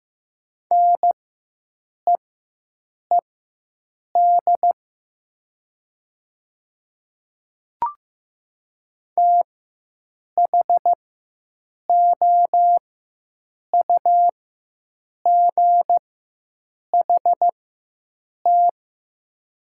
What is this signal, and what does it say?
Morse code, heard as a keyed tone, about 15 wpm, spaced as for 7 wpm, 700 Hz. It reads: NEED THOUGHT
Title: 4x Character Spacing - Top 300 Words - 15wpm